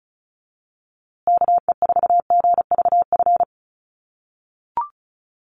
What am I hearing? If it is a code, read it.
Morse code, heard as a keyed tone, about 34 wpm, 700 Hz. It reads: KE4GVF
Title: USA Call Signs Repeated 3x 35wpm